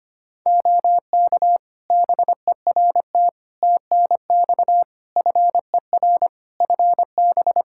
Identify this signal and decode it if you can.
Morse code, heard as a keyed tone, about 25 wpm, 700 Hz. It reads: OK BERT TNX FER FB
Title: Practice QSO - Exchange 4 - Ending - 25wpm